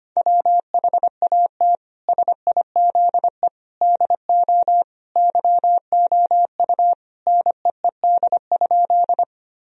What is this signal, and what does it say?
Morse code, heard as a keyed tone, about 25 wpm, 700 Hz. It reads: WHAT SIZE DO YOU NEED?
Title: Sentences from Top 1000 Words - Review Part 2 - 25wpm